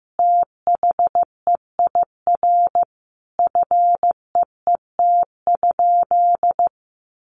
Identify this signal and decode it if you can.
Morse code, heard as a keyed tone, about 15 wpm, 700 Hz. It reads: THEIR FEET?
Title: Sentences from Top 1500 Words 15wpm